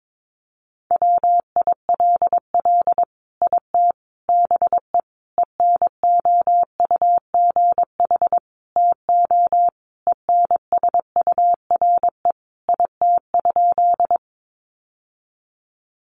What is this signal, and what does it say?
Morse code, heard as a keyed tone, about 22 wpm, 700 Hz. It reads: WILL IT BE ENOUGH TO ENSURE IT?
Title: Sentences from Top 1100 Words 22wpm